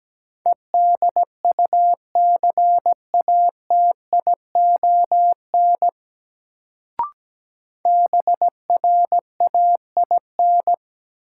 Morse code, heard as a keyed tone, about 17 wpm, 700 Hz. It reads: EDUCATION BRAIN
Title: Sentences from Top 1100 Words - New Word Drill Alt - 17wpm